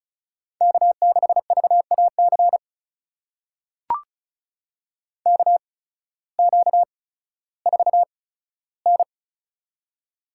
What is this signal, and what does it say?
Morse code, heard as a keyed tone, about 36 wpm, spaced as for 16 wpm, 700 Hz. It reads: K6VAC KQ4N
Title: USA Call Signs - 8x Character Spacing - 35wpm